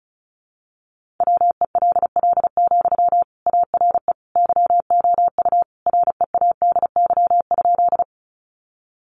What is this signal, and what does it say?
Morse code, heard as a keyed tone, about 34 wpm, 700 Hz. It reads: WELL, ARE YOU READY?
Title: Sentences from Top 400 Words 35wpm